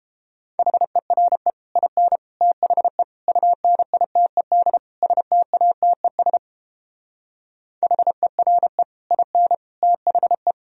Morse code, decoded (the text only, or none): HERE IN THE UNITED STATES HERE IN THE